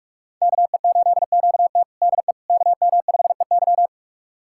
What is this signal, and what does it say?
Morse code, heard as a keyed tone, about 46 wpm, 700 Hz. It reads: KE8QT DE KM5EY
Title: USA Callsign DE Callsign - 5 Characters - 45wpm